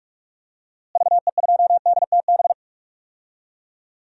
Morse code, heard as a keyed tone, about 41 wpm, 700 Hz. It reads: VE2DTB
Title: DXCC-100 Call Signs - 6 Characters - Repeated 3x - 45wpm